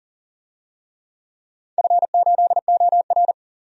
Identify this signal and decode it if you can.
Morse code, heard as a keyed tone, about 39 wpm, 690 Hz. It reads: F8OR
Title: DXCC-100 Call Signs - 4 Characters - 40wpm